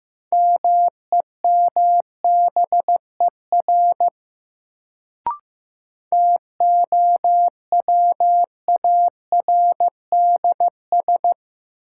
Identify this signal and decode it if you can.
Morse code, heard as a keyed tone, about 15 wpm, 700 Hz. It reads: MEMBER TOWARDS
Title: Sentences from Top 1100 Words - New Word Drill Alt - 15wpm